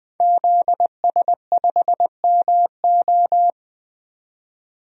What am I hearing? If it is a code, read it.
Morse code, heard as a keyed tone, about 20 wpm, 700 Hz. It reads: ZS5MO
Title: DXCC-100 Call Signs - 5 Characters - 20wpm